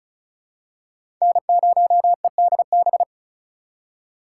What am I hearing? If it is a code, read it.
Morse code, heard as a keyed tone, about 34 wpm, 700 Hz. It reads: N0EDB
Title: USA Call Signs - 5 Characters - 3x Character Spacing - 35wpm